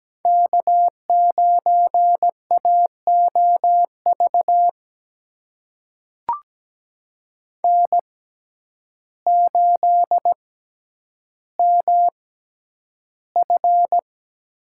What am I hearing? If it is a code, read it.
Morse code, heard as a keyed tone, about 17 wpm, spaced as for 9 wpm, 700 Hz. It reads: K9AOV N8MF
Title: USA Call Signs - 5 Characters - 6x Character Spacing - 17wpm